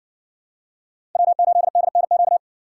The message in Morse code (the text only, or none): W8DNQ